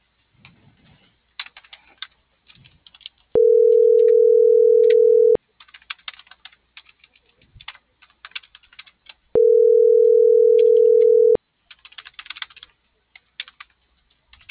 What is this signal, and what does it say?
Ringback tone (North America), 440 + 480 Hz.